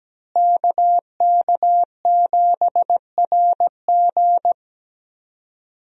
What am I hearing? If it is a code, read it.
Morse code, heard as a keyed tone, about 17 wpm, 700 Hz. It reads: KK7RG
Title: USA Call Signs - 5 Characters - Repeated 3x - 17wpm